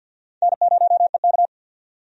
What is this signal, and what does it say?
Morse code, heard as a keyed tone, about 47 wpm, 700 Hz. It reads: N0EK
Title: USA Call Signs - 4 Characters - 50wpm